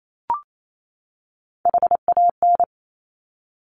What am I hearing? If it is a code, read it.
Morse code, heard as a keyed tone, about 28 wpm, 700 Hz. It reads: HAN
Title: Top Three Letter Combinations in Words 28wpm